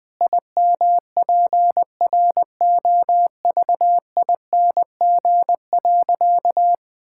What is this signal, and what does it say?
Morse code, heard as a keyed tone, about 20 wpm, 700 Hz. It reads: IMPROVING.
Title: Sentences with 8-Word Phrases - 20wpm